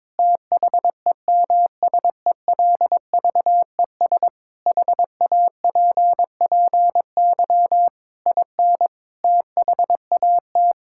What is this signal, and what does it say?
Morse code, heard as a keyed tone, about 22 wpm, 700 Hz. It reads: THEMSELVES HAPPY IN THAT
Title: Sentences from Top 1600 Words 22wpm